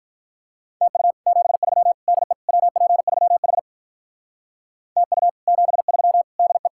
Morse code, heard as a keyed tone, about 53 wpm, 700 Hz. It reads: TU 73 DE WP3H TU 73 DE